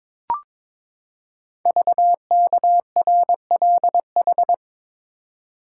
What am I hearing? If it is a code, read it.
Morse code, heard as a keyed tone, about 22 wpm, 700 Hz. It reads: VKRLH
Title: Code Groups - 5 Letters - 22wpm